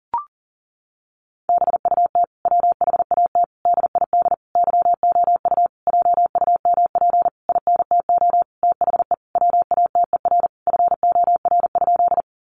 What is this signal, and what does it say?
Morse code, heard as a keyed tone, about 40 wpm, 700 Hz. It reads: BUT WHAT DID YOU JUMP INTO THE WATER FOR?
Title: Sentences from Top 1300 Words 40wpm